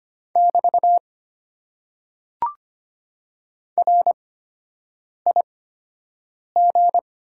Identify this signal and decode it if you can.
Morse code, heard as a keyed tone, about 25 wpm, spaced as for 7 wpm, 700 Hz. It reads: <BT> RIG
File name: Top 90 QSO Elements - 8x Character Spacing - 25wpm